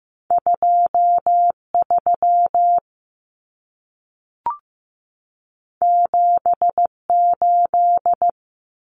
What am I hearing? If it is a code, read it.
Morse code, heard as a keyed tone, about 15 wpm, 700 Hz. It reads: 23 78
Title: Common Cut Numbers - 2-Digit 15wpm